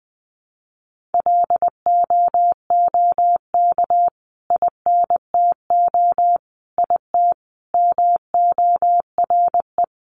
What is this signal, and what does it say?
Morse code, heard as a keyed tone, about 19 wpm, 700 Hz. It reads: LOOK INTO IT MORE
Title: Sentences from Top 1300 Words 20wpm